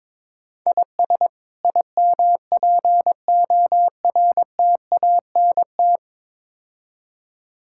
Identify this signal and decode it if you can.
Morse code, heard as a keyed tone, about 22 wpm, 700 Hz. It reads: IS IMPORTANT
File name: practice / 22wpm / Build A Phrase - 5-Word Phrases - 22wpm